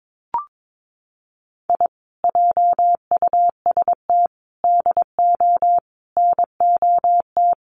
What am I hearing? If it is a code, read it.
Morse code, heard as a keyed tone, about 22 wpm, 700 Hz. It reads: I JUST DO NOT
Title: Sets of 5 Words Encore 22wpm